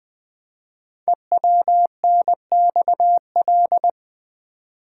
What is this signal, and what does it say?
Morse code, heard as a keyed tone, about 20 wpm, 700 Hz. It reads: EWNXL
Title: Code Groups - 5 Letters - 20wpm